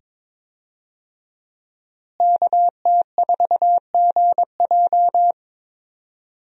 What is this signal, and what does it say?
Morse code, heard as a keyed tone, about 22 wpm, 700 Hz. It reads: KT4GJ
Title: USA Call Signs - 5 Characters - Repeated 2x - 22wpm